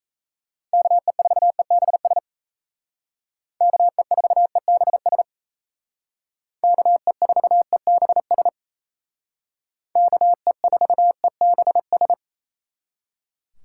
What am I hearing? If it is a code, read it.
Morse code, heard as a keyed tone, about 35 wpm, 700 Hz. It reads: KE4EBS KE4EBS KE4EBS KE4EBS